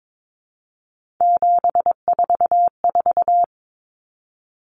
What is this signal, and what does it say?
Morse code, heard as a keyed tone, about 22 wpm, 700 Hz. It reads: 744
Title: Common Cut Numbers - 3-Digit 22wpm